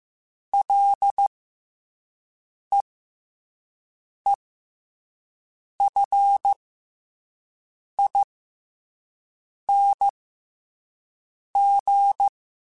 Morse code, heard as a keyed tone, about 15 wpm, spaced as for 5 wpm, 800 Hz. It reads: LEEFING
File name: F and L / F and L mp3 / F and L Prefixes 3x2 Twice @5